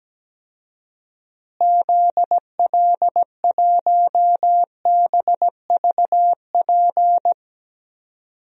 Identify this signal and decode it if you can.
Morse code, heard as a keyed tone, about 17 wpm, 700 Hz. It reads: ZL1BVP
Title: DXCC-100 Call Signs - 6 Characters - Repeated 2x - 17wpm